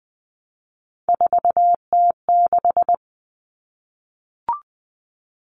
Morse code, heard as a keyed tone, about 20 wpm, 700 Hz. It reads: 4T6
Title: All Cut Numbers - 3-Digit 20wpm